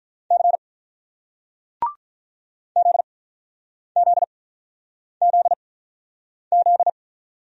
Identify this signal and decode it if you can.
Morse code, heard as a keyed tone, about 43 wpm, spaced as for 9 wpm, 700 Hz. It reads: C ZZZZ
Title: Speed-Racing Single Letters 35wpm